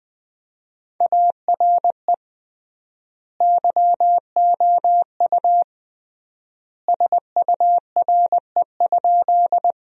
Morse code, heard as a keyed tone, about 21 wpm, 700 Hz. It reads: ARE YOU SURE?
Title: 3x Word-Spacing - Sentences from Top 300 Words - 20wpm